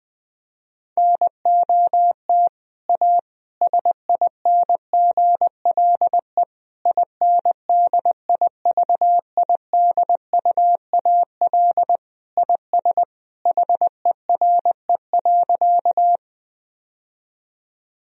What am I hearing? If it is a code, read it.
Morse code, heard as a keyed tone, about 20 wpm, 700 Hz. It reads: NOT A SINGLE INDIVIDUAL IS HERE.
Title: Sentences from Top 1000 Words - Review Part 4 - 20wpm